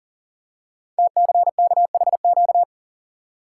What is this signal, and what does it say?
Morse code, heard as a keyed tone, about 42 wpm, 700 Hz. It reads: TCKHQ
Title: Code Groups - 5 Letters - 40wpm